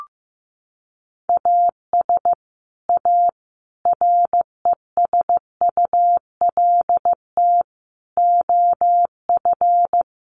Morse code, heard as a keyed tone, about 15 wpm, 700 Hz. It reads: AS A RESULT OF
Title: Sets of 5 Words Encore2 15wpm